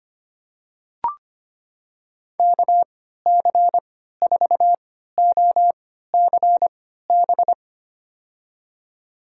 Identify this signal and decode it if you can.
Morse code, heard as a keyed tone, about 25 wpm, spaced as for 14 wpm, 700 Hz. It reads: KC4OCB
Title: USA Call Signs - 6 Characters - 3x Character Spacing - 25wpm